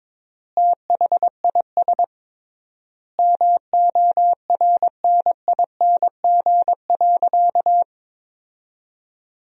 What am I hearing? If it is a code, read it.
Morse code, heard as a keyed tone, about 22 wpm, 700 Hz. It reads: THIS MORNING.
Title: 3x Word-Spacing - Sentences from Top 400 Words - 22wpm